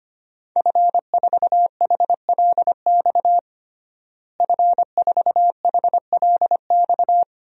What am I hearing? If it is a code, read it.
Morse code, heard as a keyed tone, about 25 wpm, 700 Hz. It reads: F4HLX F4HLX
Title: DXCC-100 Call Signs - 5 Characters - Repeated 2x - 25wpm